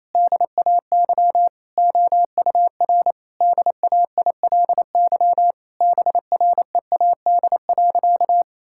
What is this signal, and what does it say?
Morse code, heard as a keyed tone, about 28 wpm, 700 Hz. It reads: DAY OUR DAILY BREAD.
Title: Sentences from Top 1600 Words 28wpm